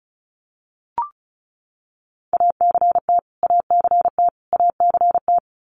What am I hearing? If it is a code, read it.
Morse code, heard as a keyed tone, about 34 wpm, 700 Hz. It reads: ACT ACT ACT
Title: Top 500 Words Repeated 35wpm